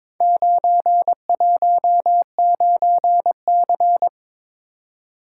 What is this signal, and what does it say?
Morse code, heard as a keyed tone, about 23 wpm, 700 Hz. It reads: 919C
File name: Four Letter-Number Combinations - 22wpm